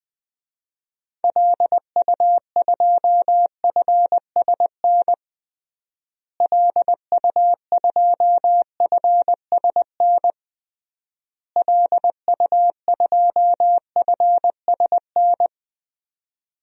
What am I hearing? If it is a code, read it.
Morse code, heard as a keyed tone, about 21 wpm, 700 Hz. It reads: LU2FSN LU2FSN LU2FSN
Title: DXCC-100 Call Signs - 6 Characters - Repeated 3x - 20wpm